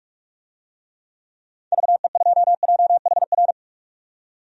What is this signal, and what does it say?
Morse code, heard as a keyed tone, about 46 wpm, 700 Hz. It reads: VE2JHR
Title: DXCC-100 Call Signs - 6 Characters - 45wpm